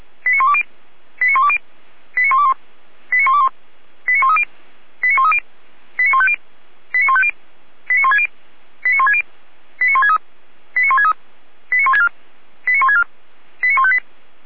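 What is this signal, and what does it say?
Five-tone selective call (ZVEI), heard as quick runs of five tones, tones of about 80 ms. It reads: 89120 89120 89121 89121 89130 89130 89150 89150 89160 89160 89163 89163 89164 89164 89158
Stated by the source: ZVEI five-tone selective call received on an SDR.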